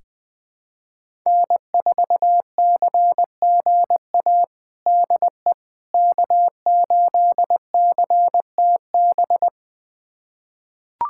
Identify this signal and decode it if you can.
Morse code, heard as a keyed tone, about 21 wpm, 700 Hz. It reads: N4CGA DE K8CTB
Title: USA Callsign DE Callsign - 5 Characters - 20wpm